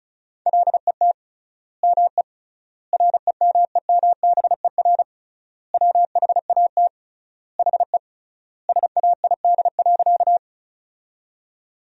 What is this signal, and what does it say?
Morse code, heard as a keyed tone, about 36 wpm, 700 Hz. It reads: LET ME REMEMBER WHAT HE SAID.
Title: 3x Word-Spacing - Sentences from Top 400 Words - 35wpm